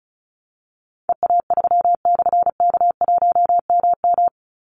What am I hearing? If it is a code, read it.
Morse code, heard as a keyed tone, about 34 wpm, 700 Hz. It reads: EA3/K1MM
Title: Call Signs - 35wpm